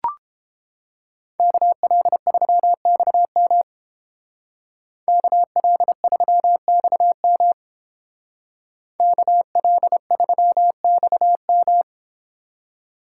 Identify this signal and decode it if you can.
Morse code, heard as a keyed tone, about 28 wpm, 700 Hz. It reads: KL3XM KL3XM KL3XM